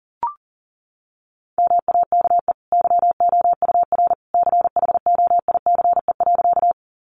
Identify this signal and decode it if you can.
Morse code, heard as a keyed tone, about 40 wpm, 700 Hz. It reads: MAKE YOUR CHOICE.
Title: Sentences from Top 900 Words 40wpm